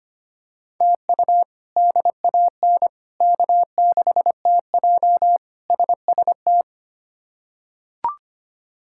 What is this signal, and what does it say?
Morse code, heard as a keyed tone, about 25 wpm, 700 Hz. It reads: TU DAN K6TJ SST
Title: K1USN SST QSO Practice - 25wpm